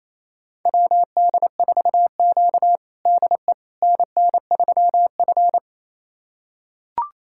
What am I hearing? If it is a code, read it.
Morse code, heard as a keyed tone, about 28 wpm, 690 Hz. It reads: WD4Q DE NN3F
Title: USA Callsign DE Callsign - 4 Characters - 28wpm